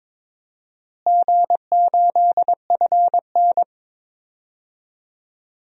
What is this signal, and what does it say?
Morse code, heard as a keyed tone, about 22 wpm, 700 Hz. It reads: G8FN
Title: DXCC-100 Call Signs - 4 Characters - Repeated 3x - 22wpm